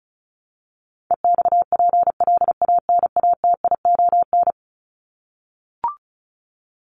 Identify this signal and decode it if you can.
Morse code, heard as a keyed tone, about 34 wpm, 700 Hz. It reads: EXPLANATION
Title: Top 1751-2000 Words 35wpm